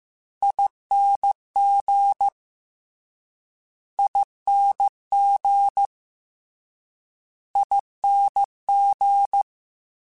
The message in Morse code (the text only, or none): ING ING ING